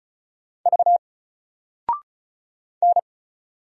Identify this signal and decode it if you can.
Morse code, heard as a keyed tone, about 36 wpm, 700 Hz. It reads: V N
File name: Single-Letter-Number-35wpm